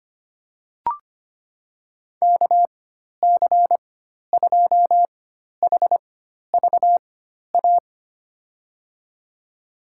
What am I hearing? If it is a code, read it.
Morse code, heard as a keyed tone, about 25 wpm, spaced as for 12 wpm, 700 Hz. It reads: KC2HVA
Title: USA Call Signs - 6 Characters - 4x Character Spacing - 25wpm